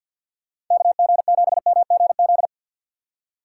Morse code, heard as a keyed tone, about 50 wpm, 700 Hz. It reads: KG7MGZ
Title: USA Call Signs - 6 Characters - 3x Character Spacing - 50wpm